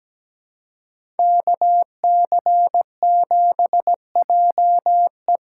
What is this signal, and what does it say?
Morse code, heard as a keyed tone, about 17 wpm, 700 Hz. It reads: KC7JE
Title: USA Call Signs - 6 Characters - 6x Character Spacing - 17wpm